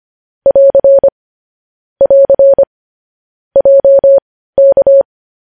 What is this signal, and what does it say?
Morse code, heard as a keyed tone, about 25 wpm, spaced as for 15 wpm, 550 Hz. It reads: <AR> <AR> JK
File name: podcast / News headlines-1x-25x15